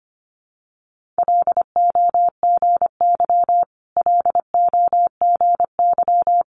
Morse code, heard as a keyed tone, about 25 wpm, 700 Hz. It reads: LOGY LOGY
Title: Multi-letter Phonemes Repeated 3x 25wpm